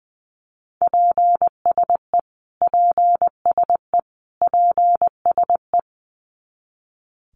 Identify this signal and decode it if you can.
Morse code, heard as a keyed tone, about 20 wpm, 700 Hz. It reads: PSE PSE PSE
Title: Abbreviations Repeated 3x - 20wpm